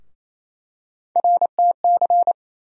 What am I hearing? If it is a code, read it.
Morse code, heard as a keyed tone, about 27 wpm, 700 Hz. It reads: RTC
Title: USA Call Sign Postfixes - 3 Letters 28wpm